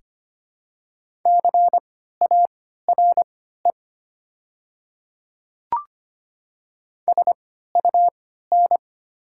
Morse code, heard as a keyed tone, about 25 wpm, spaced as for 14 wpm, 700 Hz. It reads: CARE SUN
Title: 3x Character Spacing - Top 300 Words - 25wpm